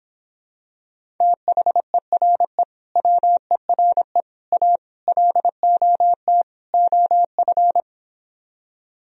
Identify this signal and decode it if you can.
Morse code, heard as a keyed tone, about 26 wpm, 700 Hz. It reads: THERE WERE A LOT OF